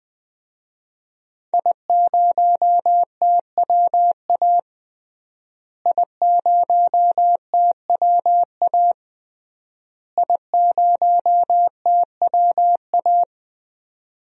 Morse code, heard as a keyed tone, about 20 wpm, 700 Hz. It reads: I0TWA I0TWA I0TWA
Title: DXCC-100 Call Signs - 5 Characters - Repeated 3x - 20wpm